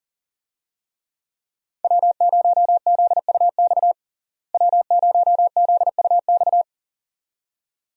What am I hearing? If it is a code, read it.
Morse code, heard as a keyed tone, about 40 wpm, 700 Hz. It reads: W0ZUX W0ZUX
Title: USA Call Signs - 5 Characters - Repeated 2x - 40wpm